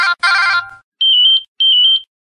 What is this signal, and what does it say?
There is no code.